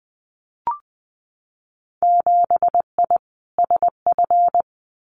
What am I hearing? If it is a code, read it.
Morse code, heard as a keyed tone, about 20 wpm, 700 Hz. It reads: 7I SF
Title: Winter Field Day Exchanges Encore 20wpm